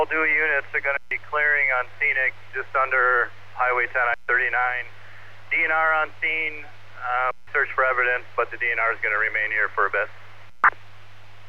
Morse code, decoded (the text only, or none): none